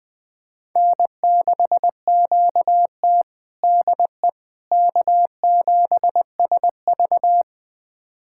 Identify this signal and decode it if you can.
Morse code, heard as a keyed tone, about 20 wpm, 700 Hz. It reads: N6QT DE K7SV
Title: USA Callsign DE Callsign - 4 Characters - 20wpm